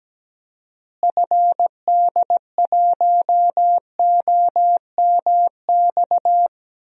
Morse code, heard as a keyed tone, about 17 wpm, 700 Hz. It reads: FD1OMX
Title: DXCC-100 Call Signs - 6 Characters - Repeated 2x - 17wpm